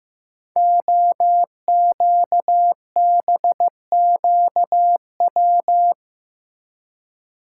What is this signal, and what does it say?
Morse code, heard as a keyed tone, about 15 wpm, 700 Hz. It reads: OQBQW
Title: Code Groups - 5 Letters - 15wpm